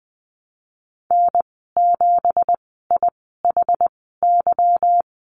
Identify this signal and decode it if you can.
Morse code, heard as a keyed tone, about 20 wpm, spaced as for 15 wpm, 700 Hz. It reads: N7IHY
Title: 173 Lesson 28g - Number 9 - Call Signs 20wpm